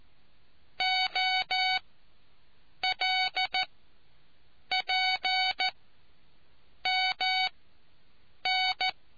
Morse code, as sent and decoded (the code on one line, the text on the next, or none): --- .-.. .--. -- -.
OLPMN